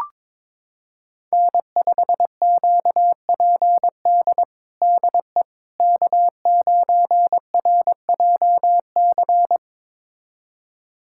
Morse code, heard as a keyed tone, about 22 wpm, 700 Hz. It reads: N5QPD DE K9RJC
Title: USA Callsign DE Callsign - 5 Characters - 22wpm